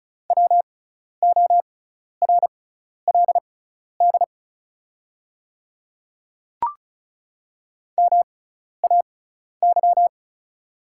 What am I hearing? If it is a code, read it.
Morse code, heard as a keyed tone, about 36 wpm, spaced as for 12 wpm, 690 Hz. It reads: WORLD MAY
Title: 6x Character Spacing - Top 300 Words - 35wpm